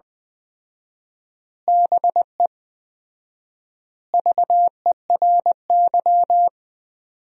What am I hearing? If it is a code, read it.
Morse code, heard as a keyed tone, about 20 wpm, 700 Hz. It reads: BE VERY
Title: 4x Word-Spacing - Sentences from Top 200 Words - 20wpm